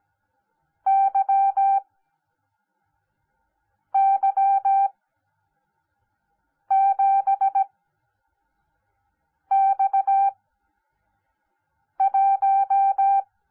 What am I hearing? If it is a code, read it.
Morse code, heard as a keyed tone, about 17 wpm, spaced as for 4 wpm, 780 Hz. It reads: YY7X1
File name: Onlinelehrgang / morsen / DJ4UF Morsen10